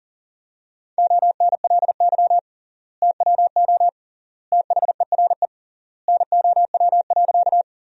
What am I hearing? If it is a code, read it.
Morse code, heard as a keyed tone, about 40 wpm, 700 Hz. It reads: ONLY TWO THERE NOW.
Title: 3x Word-Spacing - Sentences from Top 200 Words - 40wpm